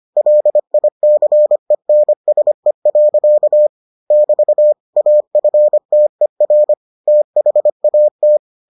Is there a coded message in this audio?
Yes: Morse code from 0.2 to 8.4 s, heard as a keyed tone, about 25 wpm, 600 Hz. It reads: LICENSE. <BT> AFTER THAT